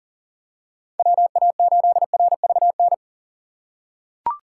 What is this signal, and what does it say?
Morse code, heard as a keyed tone, about 40 wpm, 700 Hz. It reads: WA8RVN
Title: USA Call Signs - 6 Characters - 6x Character Spacing - 40wpm